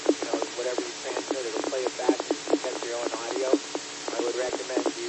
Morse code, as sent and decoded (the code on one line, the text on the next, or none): none